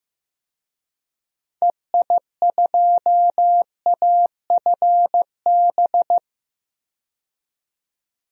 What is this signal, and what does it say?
Morse code, heard as a keyed tone, about 15 wpm, 700 Hz. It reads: EI2AFB
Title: DXCC-100 Call Signs - 6 Characters - 15wpm